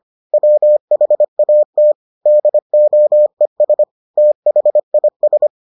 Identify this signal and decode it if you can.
Morse code, heard as a keyed tone, about 25 wpm, 600 Hz. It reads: WHAT DOES THIS